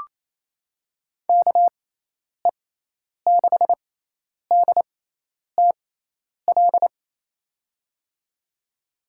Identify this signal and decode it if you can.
Morse code, heard as a keyed tone, about 27 wpm, spaced as for 10 wpm, 700 Hz. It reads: KE6DTL
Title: USA Call Signs - 6 Characters - 6x Character Spacing - 28wpm